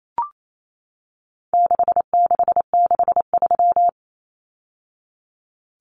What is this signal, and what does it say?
Morse code, heard as a keyed tone, about 28 wpm, 700 Hz. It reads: 6663
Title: Numbers - 4-Digit 28wpm